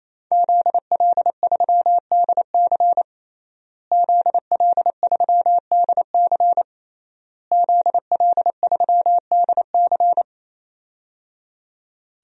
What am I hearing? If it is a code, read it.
Morse code, heard as a keyed tone, about 28 wpm, 690 Hz. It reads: ZL3DC ZL3DC ZL3DC
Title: DXCC-100 Call Signs - 5 Characters - Repeated 3x - 28wpm